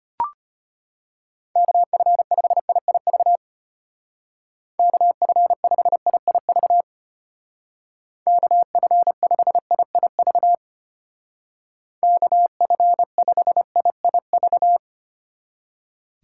Morse code, heard as a keyed tone, about 31 wpm, 700 Hz. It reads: KF5IIV KF5IIV KF5IIV KF5IIV
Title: Speed Racing - USA 2x3 Call Signs 25wpm